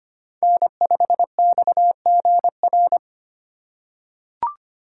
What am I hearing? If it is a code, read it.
Morse code, heard as a keyed tone, about 25 wpm, 700 Hz. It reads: N5XGR
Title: USA Call Signs - 5 Characters - 3x Character Spacing - 25wpm